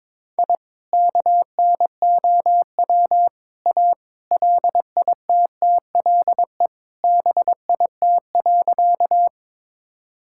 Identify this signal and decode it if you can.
Morse code, heard as a keyed tone, about 22 wpm, 700 Hz. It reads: I KNOW A LITTLE BIT.
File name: practice / 22wpm / Sentences with 3-Word Phrases - 22wpm